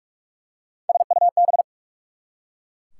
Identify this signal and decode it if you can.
Morse code, heard as a keyed tone, about 44 wpm, 690 Hz. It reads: SUB
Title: Prefixes and Suffixes 45wpm